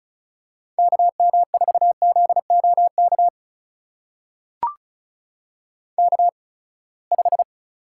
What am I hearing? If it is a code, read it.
Morse code, heard as a keyed tone, about 36 wpm, spaced as for 24 wpm, 700 Hz. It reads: KM4ZOK K5
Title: USA Call Signs - 8x Character Spacing - 35wpm